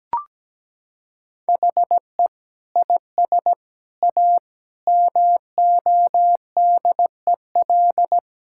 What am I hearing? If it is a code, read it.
Morse code, heard as a keyed tone, about 17 wpm, 700 Hz. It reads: HE IS A MODEL
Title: Sentences from Top 1600 Words 17wpm